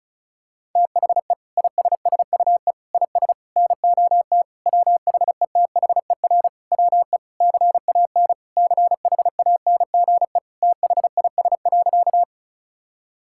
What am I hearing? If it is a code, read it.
Morse code, heard as a keyed tone, about 36 wpm, 700 Hz. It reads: THE ISSUE IS NOT WHETHER WE CAN CHANGE THIS.
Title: Sentences from Top 1000 Words - Review Part 6 - 35wpm